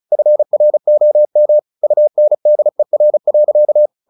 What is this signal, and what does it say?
Morse code, heard as a keyed tone, about 34 wpm, 600 Hz. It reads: FROM UNDER.
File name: the old man and the sea 2 35wpm